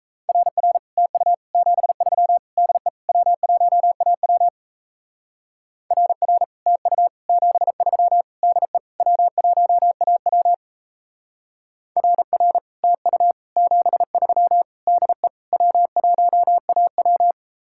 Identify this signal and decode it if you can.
Morse code, heard as a keyed tone, about 34 wpm, 700 Hz. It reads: RR TU 73 DE W1AW RR TU 73 DE W1AW RR TU 73 DE W1AW
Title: Speed-Racing Common QSO Phrases - 28wpm